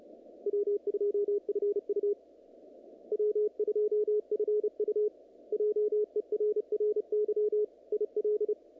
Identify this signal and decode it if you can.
Morse code, heard as a keyed tone, about 30 wpm, 440 Hz. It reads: W2FU W2FU JERRY IL
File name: contests / naqp-cw-k5tr-op-k5pi- -2200